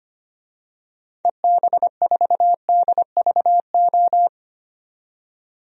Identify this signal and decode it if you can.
Morse code, heard as a keyed tone, about 25 wpm, 700 Hz. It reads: EB4DVO
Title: DXCC-100 Call Signs - 6 Characters - 25wpm